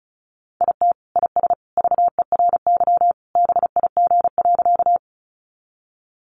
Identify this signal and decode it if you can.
Morse code, heard as a keyed tone, about 34 wpm, 700 Hz. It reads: IT IS VERY BIG.
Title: Easy Phrases 35wpm